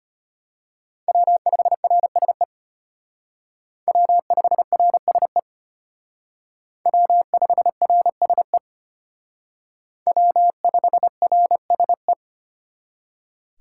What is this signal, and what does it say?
Morse code, heard as a keyed tone, about 32 wpm, 700 Hz. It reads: W5RSE W5RSE W5RSE W5RSE